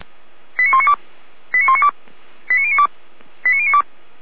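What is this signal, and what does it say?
Five-tone selective call (ZVEI), heard as quick runs of five tones, tones of about 70 ms. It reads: 89192 89192 89092 89092
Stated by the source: ZVEI five-tone selective call 89166, received on SDR.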